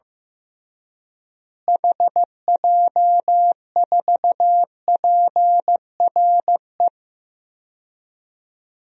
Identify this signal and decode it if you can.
Morse code, heard as a keyed tone, about 15 wpm, 700 Hz. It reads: HJ4PRE
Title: DXCC-100 Call Signs - 6 Characters - Repeated 2x - 15wpm